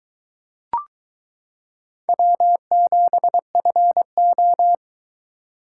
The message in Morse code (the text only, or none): W7FO